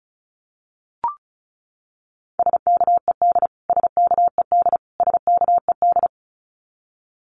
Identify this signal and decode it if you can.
Morse code, heard as a keyed tone, about 34 wpm, 700 Hz. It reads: SKED SKED SKED
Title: Abbreviations Repeated 3x - 35wpm